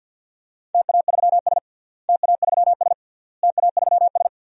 Morse code, heard as a keyed tone, about 50 wpm, 700 Hz. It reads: TA3S TA3S TA3S
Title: DXCC-100 Call Signs - 4 Characters - Repeated 3x - 50wpm